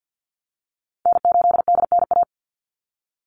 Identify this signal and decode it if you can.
Morse code, heard as a keyed tone, about 50 wpm, 700 Hz. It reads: N8DNA
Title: Call Signs Repeated 3x - 50wpm